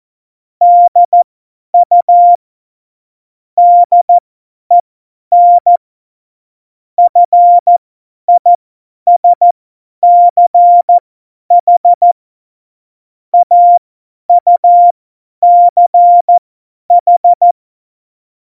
Morse code, heard as a keyed tone, about 14 wpm, spaced as for 10 wpm, 700 Hz. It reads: DU DEN FISCH AUCH
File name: Morsekurs / Witze / Witz 19. November 2019, 50 F70